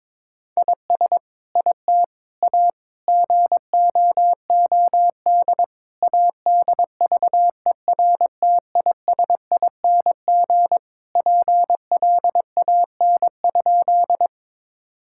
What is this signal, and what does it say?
Morse code, heard as a keyed tone, about 22 wpm, 700 Hz. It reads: IS IT A GOOD ADVERTISING PLAN?
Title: Sentences from Top 1000 Words - Review Part 6 - 22wpm